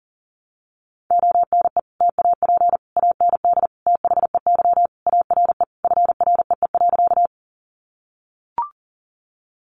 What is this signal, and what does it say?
Morse code, heard as a keyed tone, about 40 wpm, 700 Hz. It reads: ONE TAP AND THEY ARE FREE.
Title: Sentences from Top 1400 Words 40wpm